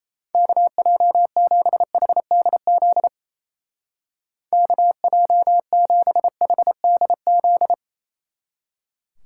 Morse code, heard as a keyed tone, about 36 wpm, 700 Hz. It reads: KJ7HDZ KJ7HDZ